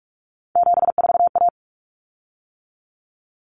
Morse code, heard as a keyed tone, about 44 wpm, 700 Hz. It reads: 74A
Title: Common Cut Numbers - 3-Digit 45wpm